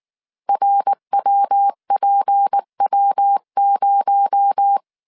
Morse code, heard as a keyed tone, about 19 wpm, 800 Hz. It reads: RWPW0
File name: CW / DET / 1 6 1500C S C 18wpm 2356